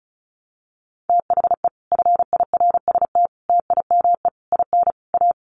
Morse code, heard as a keyed tone, about 34 wpm, 700 Hz. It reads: THE FIRST TIME IN A
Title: Sets of 5 Words Encore 35wpm